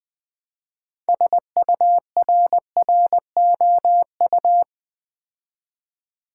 Morse code, heard as a keyed tone, about 20 wpm, 700 Hz. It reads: SURROU
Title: Build A Word - 2 Letters At A Time - 8-Letter Words - 20wpm